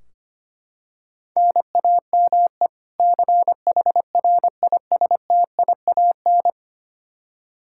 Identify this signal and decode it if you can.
Morse code, heard as a keyed tone, about 25 wpm, 700 Hz. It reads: NAME CHRISTIAN
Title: Practice QSO - Exchange 1 - Beginning - 25wpm